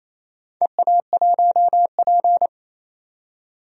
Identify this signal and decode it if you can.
Morse code, heard as a keyed tone, about 28 wpm, 700 Hz. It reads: EA1P
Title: DXCC-100 Call Signs - 4 Characters - 28wpm